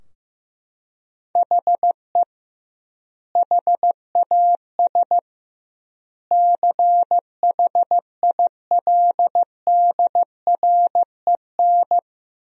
Morse code, heard as a keyed tone, about 15 wpm, 700 Hz. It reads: HE HAS CHILDREN
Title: 2x Word-Spacing - Sentences from Top 300 Words - 15wpm